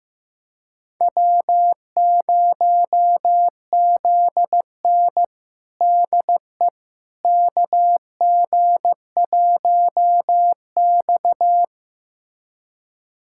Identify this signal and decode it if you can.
Morse code, heard as a keyed tone, about 15 wpm, 700 Hz. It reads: W0ZN DE KG1X
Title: USA Callsign DE Callsign - 4 Characters - 15wpm